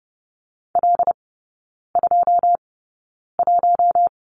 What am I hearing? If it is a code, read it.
Morse code, heard as a keyed tone, about 30 wpm, spaced as for 9 wpm, 700 Hz. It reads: L21
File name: Lesson 16e - Letter c - Three Characters 30wpm